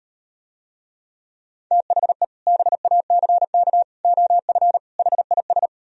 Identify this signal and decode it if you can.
Morse code, heard as a keyed tone, about 38 wpm, 700 Hz. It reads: THE BACK OF HIS